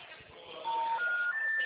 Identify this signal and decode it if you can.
Special information tone, with noise.
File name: out- -2015- - - .59831